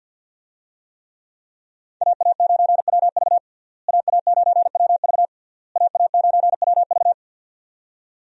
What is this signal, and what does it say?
Morse code, heard as a keyed tone, about 51 wpm, 700 Hz. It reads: AA9WV AA9WV AA9WV
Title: USA Call Signs - 5 Characters - Repeated 3x - 50wpm